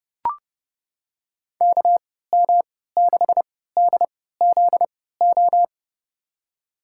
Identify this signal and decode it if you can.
Morse code, heard as a keyed tone, about 30 wpm, spaced as for 17 wpm, 700 Hz. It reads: KM6DZO
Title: USA Call Signs - 6 Characters - 3x Character Spacing - 30wpm